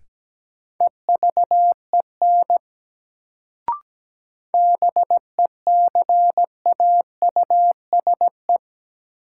Morse code, heard as a keyed tone, about 17 wpm, 700 Hz. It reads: EVEN BECAUSE
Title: Sentences from Top 600 Words - New Word Drill - 17wpm